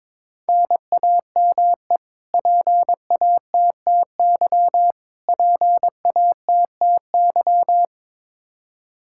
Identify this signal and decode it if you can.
Morse code, heard as a keyed tone, about 22 wpm, 700 Hz. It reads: NAME PATTY PATTY
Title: Practice QSO - Exchange 1 - Beginning - 22wpm